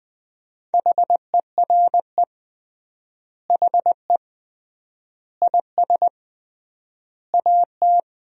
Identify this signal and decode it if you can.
Morse code, heard as a keyed tone, about 20 wpm, 700 Hz. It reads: HERE HE IS AT
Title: 3x Word-Spacing - Sentences from Top 300 Words - 20wpm